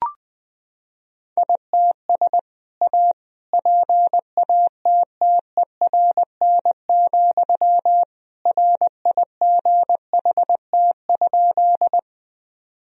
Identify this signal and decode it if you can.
Morse code, heard as a keyed tone, about 19 wpm, 700 Hz. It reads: ITS A PATTERN, RIGHT?
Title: Sentences from Top 1000 Words - Review Part 2 - 20wpm